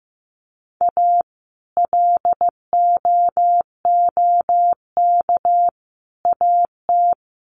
Morse code, heard as a keyed tone, about 15 wpm, 700 Hz. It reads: A LOOK AT
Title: Sets of 5 Words Encore2 15wpm